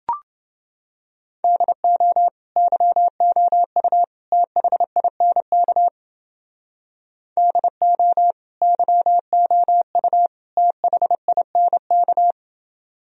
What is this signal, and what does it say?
Morse code, heard as a keyed tone, about 29 wpm, 700 Hz. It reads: DO YOU THINK DO YOU THINK